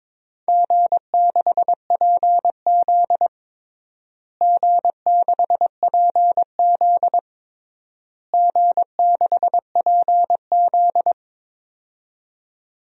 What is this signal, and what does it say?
Morse code, heard as a keyed tone, about 21 wpm, 700 Hz. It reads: G6PZ G6PZ G6PZ
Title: DXCC-100 Call Signs - 4 Characters - Repeated 3x - 22wpm